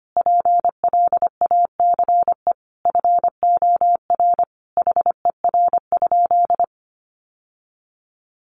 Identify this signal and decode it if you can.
Morse code, heard as a keyed tone, about 25 wpm, 700 Hz. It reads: PLACE FOR HER?
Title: Sentences from Top 800 Words 25wpm